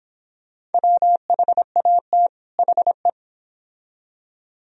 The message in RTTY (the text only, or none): none